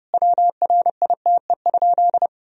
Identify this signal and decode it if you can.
Morse code, heard as a keyed tone, about 32 wpm, 700 Hz. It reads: WRITE?
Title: Sentences from Top 1000 Words - Review Part 1 - 30wpm